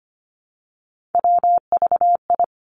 Morse code, heard as a keyed tone, about 25 wpm, 700 Hz. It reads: WVI
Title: Three Challenging Letter-Number Combinations 25wpm